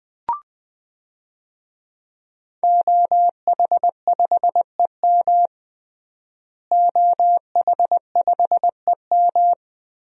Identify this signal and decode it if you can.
Morse code, heard as a keyed tone, about 20 wpm, 700 Hz. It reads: OH5EM OH5EM
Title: DXCC-100 Call Signs - 5 Characters - Repeated 3x - 20wpm